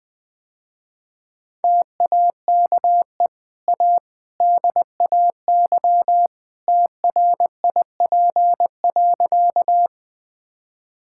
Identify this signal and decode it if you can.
Morse code, heard as a keyed tone, about 20 wpm, 700 Hz. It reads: TAKE A DAY TRIP.
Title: Easy Sentences with Hints - 20wpm